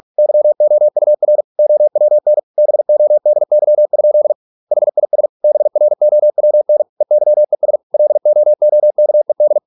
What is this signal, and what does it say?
Morse code, heard as a keyed tone, about 46 wpm, 600 Hz. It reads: YOUR OWN BODY? HIS BROWN EYES LOOKED